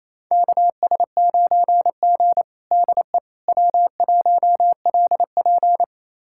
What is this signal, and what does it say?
Morse code, heard as a keyed tone, about 28 wpm, 700 Hz. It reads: KS9G DE W1LP
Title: USA Callsign DE Callsign - 4 Characters - 28wpm